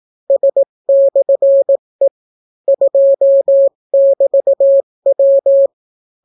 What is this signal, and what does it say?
Morse code, heard as a keyed tone, about 18 wpm, 550 Hz. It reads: S/E 2<BT>W